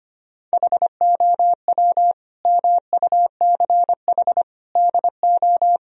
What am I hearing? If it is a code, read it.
Morse code, heard as a keyed tone, about 25 wpm, 700 Hz. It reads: HOW MUCH DO
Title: Sentences from Top 1000 Words - Review Part 3 - 25wpm